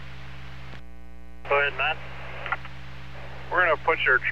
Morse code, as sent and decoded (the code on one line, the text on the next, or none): none